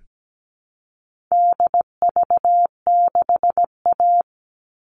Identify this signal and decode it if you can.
Morse code, heard as a keyed tone, about 17 wpm, 700 Hz. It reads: DV6A
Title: All Cut Numbers - 4-Digit 17wpm